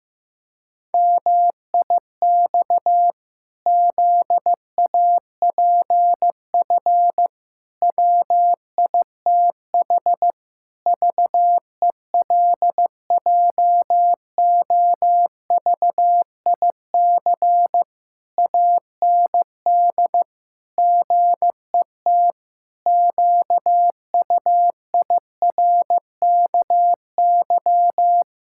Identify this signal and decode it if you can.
Morse code, heard as a keyed tone, about 15 wpm, 700 Hz. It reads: MIX ZAPF WITH VELJOVIC AND GET QUIRKY
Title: Pangram Challenge v2 - 15wpm